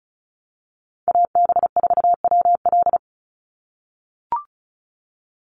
Morse code, heard as a keyed tone, about 34 wpm, 700 Hz. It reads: AB4WL
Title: NA SOTA Chaser Callsigns 35wpm